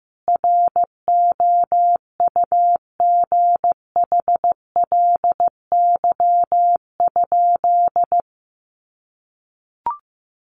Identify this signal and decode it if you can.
Morse code, heard as a keyed tone, about 15 wpm, 700 Hz. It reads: ROUGHLY?
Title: Sentences from Top 1500 Words 15wpm